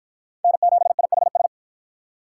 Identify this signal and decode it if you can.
Morse code, heard as a keyed tone, about 48 wpm, 700 Hz. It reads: N7IHS